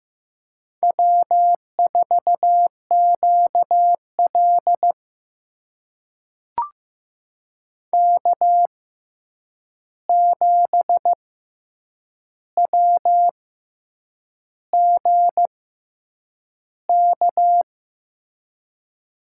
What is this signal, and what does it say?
Morse code, heard as a keyed tone, about 16 wpm, spaced as for 7 wpm, 700 Hz. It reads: W4QL K7WGK
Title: USA Call Signs - 6x Character Spacing - 15wpm